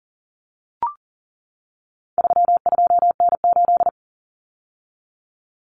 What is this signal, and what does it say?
Morse code, heard as a keyed tone, about 40 wpm, 700 Hz. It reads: 32N8
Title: Common Cut Numbers - 4-Digit 40wpm